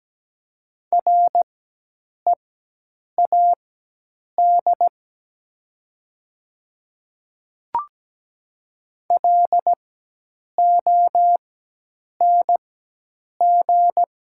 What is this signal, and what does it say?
Morse code, heard as a keyed tone, about 17 wpm, spaced as for 8 wpm, 700 Hz. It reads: READ LONG
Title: 4x Character Spacing - Top 200 Words - 17wpm